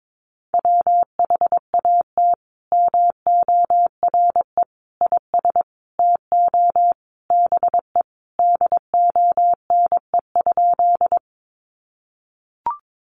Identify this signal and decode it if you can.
Morse code, heard as a keyed tone, about 22 wpm, 700 Hz. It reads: WHAT MORE IS TO BE DONE?
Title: Sentences from Top 500 Words 22wpm